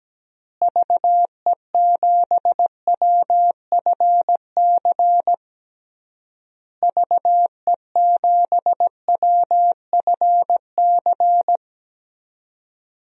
Morse code, heard as a keyed tone, about 17 wpm, 700 Hz. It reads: VE7WFC VE7WFC
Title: DXCC-100 Call Signs - 6 Characters - Repeated 3x - 17wpm